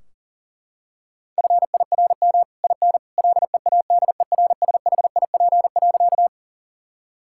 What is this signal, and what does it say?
Morse code, heard as a keyed tone, about 37 wpm, 700 Hz. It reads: FIRM IN LEADERSHIP.
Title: Sentences from Top 1600 Words 40wpm